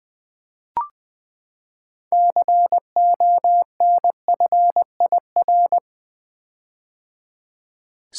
Morse code, heard as a keyed tone, about 20 wpm, 700 Hz. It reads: CONFIR
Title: Build A Word - 6 Letters At A Time - 12-Letter Words - 20wpm